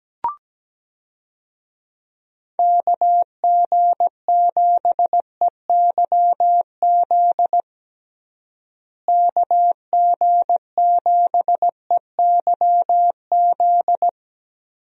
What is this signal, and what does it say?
Morse code, heard as a keyed tone, about 17 wpm, 700 Hz. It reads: KG7EYZ KG7EYZ
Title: USA Call Signs - 6 Characters - Repeated 3x - 17wpm